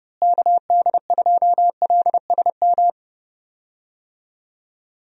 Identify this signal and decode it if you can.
Morse code, heard as a keyed tone, about 30 wpm, 700 Hz. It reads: KD2LSM
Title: USA Call Signs - 6 Characters - Repeated 3x - 30wpm